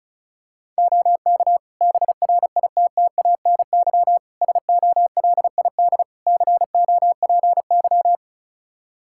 Morse code, heard as a keyed tone, about 36 wpm, 700 Hz. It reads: OK BRITTANY SOLID COPY
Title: K1USN SST QSO Practice 35wpm